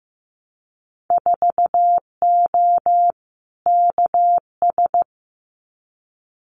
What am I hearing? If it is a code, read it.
Morse code, heard as a keyed tone, about 15 wpm, 700 Hz. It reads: 4O KS
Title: Winter Field Day Exchanges Encore 15wpm